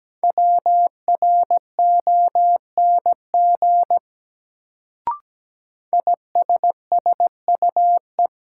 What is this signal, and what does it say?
Morse code, heard as a keyed tone, about 17 wpm, 700 Hz. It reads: WRONG ISSUE
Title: Sentences from Top 900 Words - New Word Drill - 17wpm